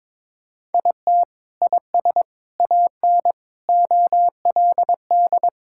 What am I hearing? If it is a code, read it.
Morse code, heard as a keyed tone, about 22 wpm, 700 Hz. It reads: IT IS AN OLD
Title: Sentences from Top 1600 Words 22wpm